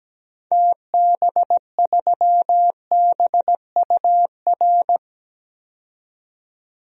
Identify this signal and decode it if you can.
Morse code, heard as a keyed tone, about 17 wpm, 700 Hz. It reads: TB3BUR
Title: DXCC-100 Call Signs - 6 Characters - 17wpm